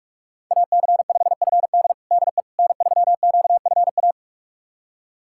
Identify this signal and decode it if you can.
Morse code, heard as a keyed tone, about 47 wpm, 700 Hz. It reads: AC5FD DE N3QFA
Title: USA Callsign DE Callsign - 5 Characters - 45wpm